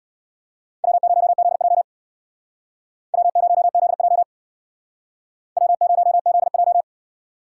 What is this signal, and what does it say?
Morse code, heard as a keyed tone, about 56 wpm, 700 Hz. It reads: W0ZJ W0ZJ W0ZJ
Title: Speed Racing - USA 1x2 Call Signs 50wpm